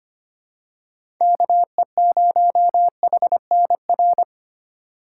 Morse code, heard as a keyed tone, about 25 wpm, 700 Hz. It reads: KE0HNR
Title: USA Call Signs - 6 Characters - 2x Character Spacing - 25wpm